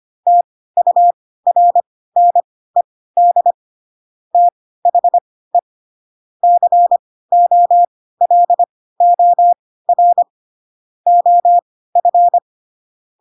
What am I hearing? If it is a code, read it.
Morse code, heard as a keyed tone, about 25 wpm, spaced as for 16 wpm, 700 Hz. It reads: TURNED THE COLOR OF
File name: Pinocchio / Pinocchio ch2 10 25 ik1hge